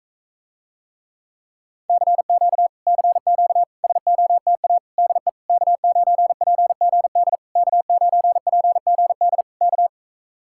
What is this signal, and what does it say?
Morse code, heard as a keyed tone, about 45 wpm, 700 Hz. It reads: CQ CQ SOTA DE K9PGD K9PGD K